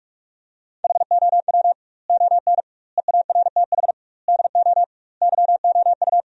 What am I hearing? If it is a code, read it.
Morse code, heard as a keyed tone, about 46 wpm, 700 Hz. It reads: HOW ON EARTH DO YOU
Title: 2x Word-Spacing - Sentences from Top 200 Words - 45wpm